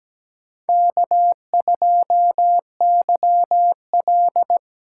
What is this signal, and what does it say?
Morse code, heard as a keyed tone, about 17 wpm, 700 Hz. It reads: K2YL
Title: USA Call Signs - 3x Character Spacing - 17wpm